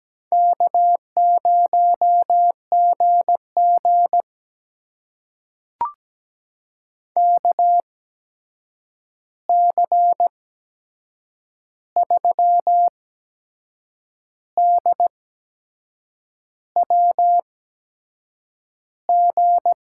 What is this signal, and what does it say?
Morse code, heard as a keyed tone, about 17 wpm, spaced as for 6 wpm, 700 Hz. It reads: K0GG KC3DWG
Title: USA Call Signs - 8x Character Spacing - 17wpm